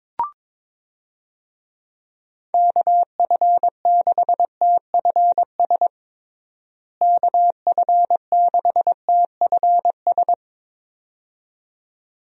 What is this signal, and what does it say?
Morse code, heard as a keyed tone, about 21 wpm, 700 Hz. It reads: KF6TFS KF6TFS
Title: USA Call Signs - 6 Characters - Repeated 2x - 22wpm